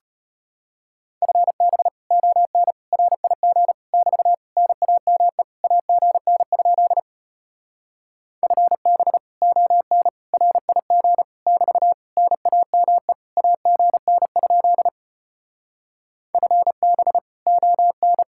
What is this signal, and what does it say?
Morse code, heard as a keyed tone, about 33 wpm, 700 Hz. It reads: FB ON RIG <BT> NAME AGN? FB ON RIG <BT> NAME AGN? FB ON